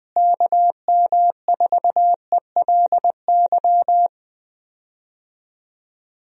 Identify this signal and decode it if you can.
Morse code, heard as a keyed tone, about 20 wpm, 700 Hz. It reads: KM4ELY
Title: USA Call Signs - 6 Characters - Repeated 2x - 20wpm